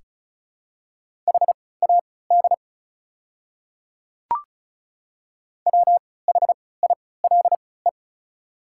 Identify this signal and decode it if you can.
Morse code, heard as a keyed tone, about 34 wpm, spaced as for 20 wpm, 700 Hz. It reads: HAD WHILE
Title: 3x Character Spacing - Top 300 Words - 35wpm